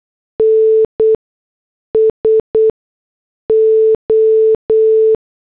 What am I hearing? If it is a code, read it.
Morse code, heard as a keyed tone, about 8 wpm, spaced as for 6 wpm, 440 Hz. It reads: NSO